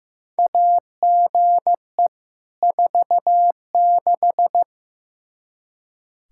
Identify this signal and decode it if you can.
Morse code, heard as a keyed tone, about 15 wpm, 700 Hz. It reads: AGE 46
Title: Practice QSO - Exchange 3 - More Info - 15wpm